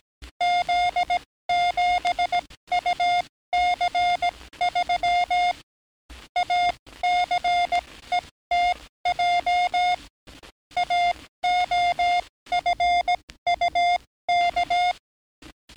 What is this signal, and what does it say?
Morse code, heard as a keyed tone, about 17 wpm, spaced as for 14 wpm, 710 Hz. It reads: Z7UC3 ACETJ AOFUK
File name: training / NAVPERS 11049to11060 / NAVPERS 11057 RA RB